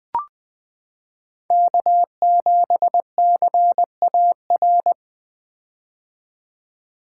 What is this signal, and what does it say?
Morse code, heard as a keyed tone, about 20 wpm, 700 Hz. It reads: K7CAR
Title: 2023 POTA Activator Call Signs - 20wpm